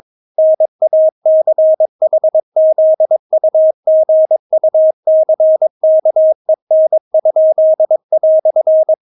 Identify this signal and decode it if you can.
Morse code, heard as a keyed tone, about 22 wpm, 630 Hz. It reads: NACHZUGUCKEN?"
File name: Witz3 22wpm